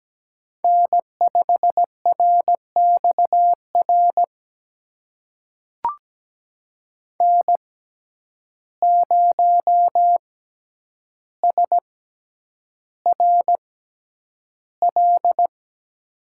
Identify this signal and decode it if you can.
Morse code, heard as a keyed tone, about 17 wpm, spaced as for 9 wpm, 700 Hz. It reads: N5RXR N0SRL
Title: USA Call Signs - 5 Characters - 6x Character Spacing - 17wpm